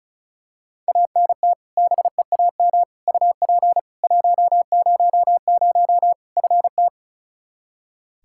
Morse code, heard as a keyed tone, about 34 wpm, 700 Hz. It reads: ANT BEAM UP 100 FT
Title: K1USN SST QSO Practice 35wpm